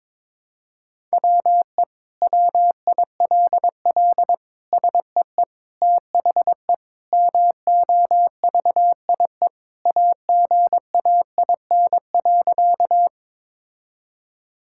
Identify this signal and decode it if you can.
Morse code, heard as a keyed tone, about 22 wpm, 700 Hz. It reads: WE WILL SEE THE MOVIE AGAIN.
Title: Sentences from Top 1000 Words - Review Part 5 - 22wpm